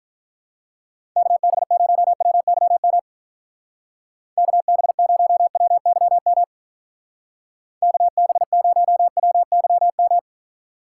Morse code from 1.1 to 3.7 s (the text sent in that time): KB0WYM